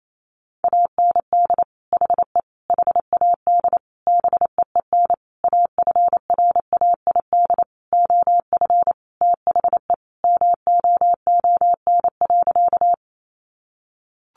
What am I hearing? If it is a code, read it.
Morse code, heard as a keyed tone, about 28 wpm, 700 Hz. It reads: AND HE HAD BEEN AFRAID OF THE MOON.
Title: Sentences from Top 1500 Words 28wpm